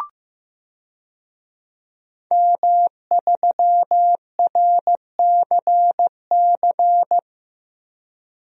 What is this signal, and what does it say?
Morse code, heard as a keyed tone, about 15 wpm, 700 Hz. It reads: M3RCC
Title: DXCC-100 Call Signs - 5 Characters - Repeated 2x - 15wpm